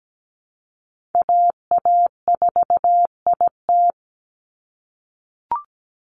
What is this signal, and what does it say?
Morse code, heard as a keyed tone, about 17 wpm, 700 Hz. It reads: AA4IT
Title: NA SOTA Chaser Callsigns 17wpm